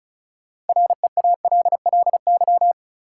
Morse code, heard as a keyed tone, about 36 wpm, 700 Hz. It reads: REALLY
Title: Sentences from Top 600 Words - New Word Drill - 35wpm